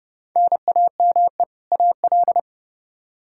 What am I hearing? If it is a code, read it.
Morse code, heard as a keyed tone, about 30 wpm, 700 Hz. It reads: NAME AL
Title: K1USN SST QSO Practice 30wpm